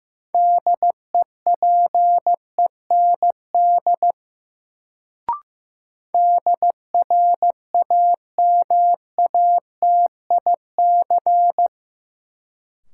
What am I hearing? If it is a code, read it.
Morse code, heard as a keyed tone, about 15 wpm, 700 Hz. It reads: DEPEND DRAMATIC
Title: Sentences from Top 1300 Words - New Word Drill Alt - 15wpm